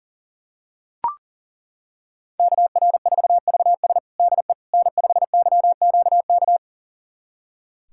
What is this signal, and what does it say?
Morse code, heard as a keyed tone, about 40 wpm, 700 Hz. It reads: KR4VS DE N5YQK
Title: USA Callsign DE Callsign - 5 Characters - 40wpm